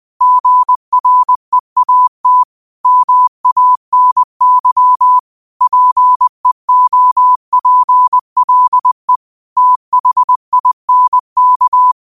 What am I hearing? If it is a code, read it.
Morse code, heard as a keyed tone, about 20 wpm, 1000 Hz. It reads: GREAT MANY PEOPLE THINK
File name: qotdfiles / 20wpm